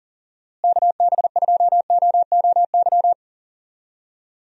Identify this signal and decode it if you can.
Morse code, heard as a keyed tone, about 40 wpm, 700 Hz. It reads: KB2OOY
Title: USA Call Signs - 6 Characters - Repeated 2x - 40wpm